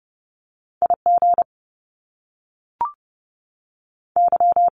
Morse code, heard as a keyed tone, about 30 wpm, 700 Hz. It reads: IG Y
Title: Top Two Letter Combinations in Words 30wpm